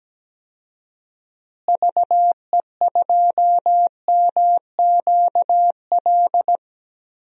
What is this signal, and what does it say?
Morse code, heard as a keyed tone, about 17 wpm, 700 Hz. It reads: VE2MQL
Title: DXCC-100 Call Signs - 6 Characters - Repeated 3x - 17wpm